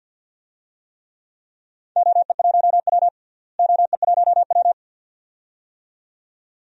Morse code, heard as a keyed tone, about 47 wpm, 700 Hz. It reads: OE1W OE1W
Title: DXCC-100 Call Signs - 4 Characters - Repeated 2x - 50wpm